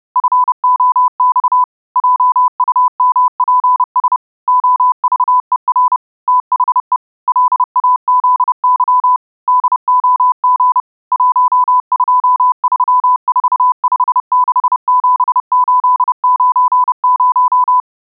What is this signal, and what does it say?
Morse code, heard as a keyed tone, about 30 wpm, 1000 Hz. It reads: FOX JUMPS OVER THE LAZY DOG 1234567890